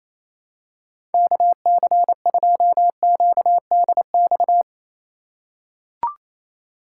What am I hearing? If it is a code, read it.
Morse code, heard as a keyed tone, about 28 wpm, 700 Hz. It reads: KC2QDX
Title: USA Call Signs - 6 Characters - 4x Character Spacing - 28wpm